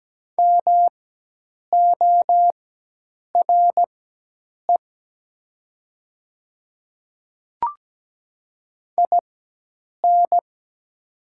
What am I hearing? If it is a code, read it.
Morse code, heard as a keyed tone, about 17 wpm, spaced as for 8 wpm, 700 Hz. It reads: MORE IN
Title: 4x Character Spacing - Top 200 Words - 17wpm